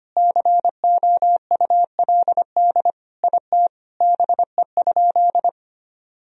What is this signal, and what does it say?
Morse code, heard as a keyed tone, about 25 wpm, 700 Hz. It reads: COULD IT BE?
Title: Sentences from Top 1000 Words - Review Part 2 - 25wpm